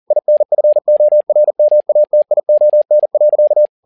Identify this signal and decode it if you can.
Morse code, heard as a keyed tone, about 40 wpm, 600 Hz. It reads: INFORMATION.
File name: learning center / tower trouble / tower touble 40wpm